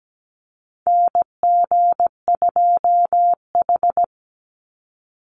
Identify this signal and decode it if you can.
Morse code, heard as a keyed tone, about 17 wpm, 700 Hz. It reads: NG2H
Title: Call Signs Repeated 2x - 17wpm